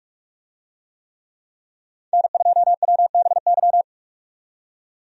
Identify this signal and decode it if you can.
Morse code, heard as a keyed tone, about 42 wpm, 700 Hz. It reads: N2WBY
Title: USA Call Signs - 5 Characters - 45wpm